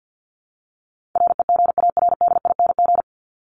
Morse code, heard as a keyed tone, about 50 wpm, 700 Hz. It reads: REGARDING
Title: ING Suffix - Top 100 Words with Base Words 50wpm